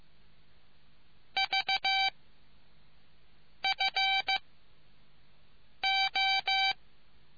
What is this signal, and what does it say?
Morse code, heard as a keyed tone, about 15 wpm, spaced as for 5 wpm, 2300 Hz. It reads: V FO